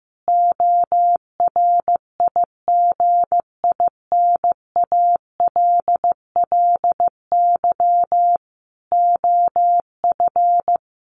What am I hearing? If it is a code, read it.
Morse code, heard as a keyed tone, about 14 wpm, 700 Hz. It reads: ORIGINALLY OF
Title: Sentences from Top 1500 Words 15wpm